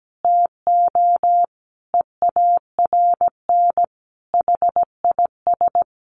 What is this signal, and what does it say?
Morse code, heard as a keyed tone, about 17 wpm, 700 Hz. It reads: TO EARN HIS
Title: Sentences from Top 1500 Words 17wpm